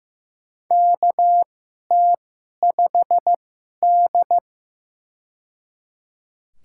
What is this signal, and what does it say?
Morse code, heard as a keyed tone, about 16 wpm, spaced as for 11 wpm, 700 Hz. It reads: KT5D
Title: USA Call Signs - 4 Characters - 2x Character Spacing - 15wpm